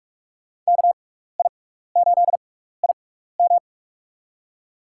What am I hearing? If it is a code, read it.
Morse code, heard as a keyed tone, about 45 wpm, spaced as for 15 wpm, 690 Hz. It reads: KI8IM
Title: USA Call Signs - 5 Characters - 6x Character Spacing - 45wpm